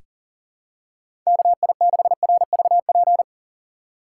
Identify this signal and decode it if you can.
Morse code, heard as a keyed tone, about 37 wpm, 700 Hz. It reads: KI6RVP
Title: USA Call Signs - 3x Character Spacing - 40wpm